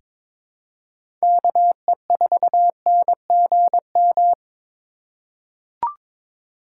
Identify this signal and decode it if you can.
Morse code, heard as a keyed tone, about 22 wpm, 700 Hz. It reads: KE4NGM
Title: USA Call Signs - 6 Characters - 3x Character Spacing - 22wpm